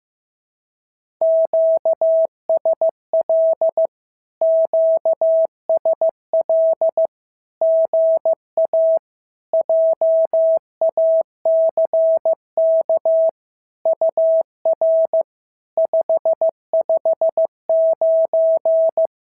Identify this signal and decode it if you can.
Morse code, heard as a keyed tone, about 15 wpm, 650 Hz. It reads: QSL QSL GA JACK UR 559
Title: SOTA QSO Practice 15wpm